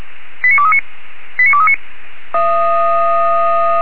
Five-tone selective call (ZVEI), heard as quick runs of five tones, tones of about 70 ms. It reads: 89239 89239